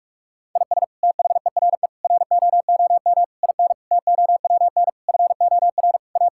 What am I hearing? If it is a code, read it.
Morse code, heard as a keyed tone, about 41 wpm, 700 Hz. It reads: IS THERE ROOM IN TOWN FOR A
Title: Sentences from Top 1000 Words - Review Part 5 - 45wpm